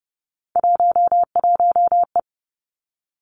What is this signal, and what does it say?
Morse code, heard as a keyed tone, about 30 wpm, 700 Hz. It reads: 11E
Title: Three Challenging Letter-Number Combinations 30wpm